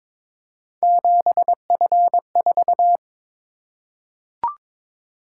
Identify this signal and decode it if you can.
Morse code, heard as a keyed tone, about 21 wpm, 700 Hz. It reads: 7F4
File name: practice / 22wpm / Three Letter-Number Combinations - 22wpm